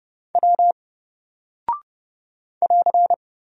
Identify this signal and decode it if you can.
Morse code, heard as a keyed tone, about 29 wpm, 700 Hz. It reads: W <AR>
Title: Single Character (Letter:Number:Punctuation:Prosign) 30wpm